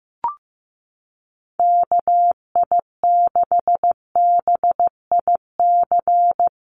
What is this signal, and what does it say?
Morse code, heard as a keyed tone, about 15 wpm, 700 Hz. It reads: KI6BIC
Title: USA Call Signs Repeated 3x 15wpm